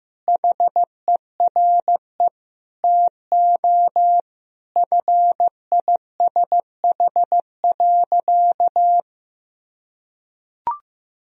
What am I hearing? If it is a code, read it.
Morse code, heard as a keyed tone, about 15 wpm, 700 Hz. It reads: HERE TO FISH.
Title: Sentences from Top 1000 Words - Review Part 1 - 15wpm